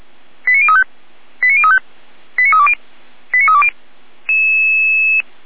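Five-tone selective call (ZVEI), heard as quick runs of five tones, tones of about 70 ms. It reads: 89035 89035 89320 89320
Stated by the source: ZVEI five-tone call received by SDR.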